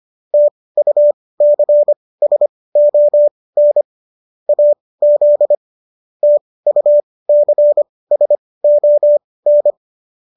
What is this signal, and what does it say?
Morse code, heard as a keyed tone, about 25 wpm, spaced as for 18 wpm, 590 Hz. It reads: TUCSON AZ TUCSON